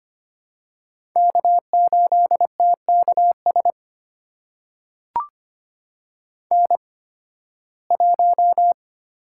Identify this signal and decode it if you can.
Morse code, heard as a keyed tone, about 25 wpm, spaced as for 16 wpm, 700 Hz. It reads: K8TKS N1
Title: USA Call Signs - 5 Characters - 8x Character Spacing - 25wpm